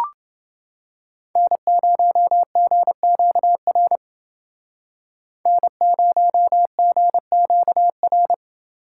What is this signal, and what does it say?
Morse code, heard as a keyed tone, about 29 wpm, 700 Hz. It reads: N0GQR N0GQR